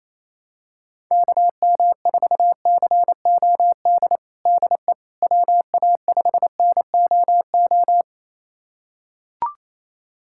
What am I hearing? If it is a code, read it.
Morse code, heard as a keyed tone, about 28 wpm, 700 Hz. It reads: KM4COD DE WA5NOO
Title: USA Callsign DE Callsign - 6 Characters - 28wpm